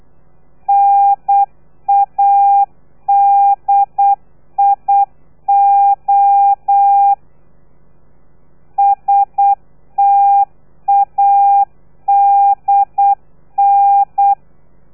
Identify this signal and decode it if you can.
Morse code, heard as a keyed tone, about 8 wpm, 800 Hz. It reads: NADIO STADN
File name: telegrafia / i2tfj / Lez3 35